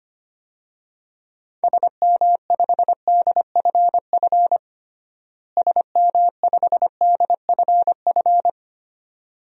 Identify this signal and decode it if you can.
Morse code, heard as a keyed tone, about 25 wpm, 700 Hz. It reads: SM5DFF SM5DFF
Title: DXCC-100 Call Signs - 6 Characters - Repeated 2x - 25wpm